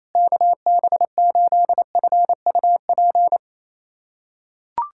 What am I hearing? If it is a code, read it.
Morse code, heard as a keyed tone, about 28 wpm, 700 Hz. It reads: KB8FUP
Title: USA Callsign DE Callsign 28wpm